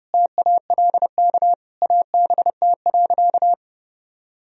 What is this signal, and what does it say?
Morse code, heard as a keyed tone, about 30 wpm, 700 Hz. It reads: TALK ABT.
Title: Sentences from Top 400 Words and Abbreviations 30wpm